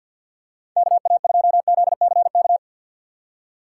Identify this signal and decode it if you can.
Morse code, heard as a keyed tone, about 50 wpm, 700 Hz. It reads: KA2ZCK
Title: USA Call Signs - 4x Character Spacing - 50wpm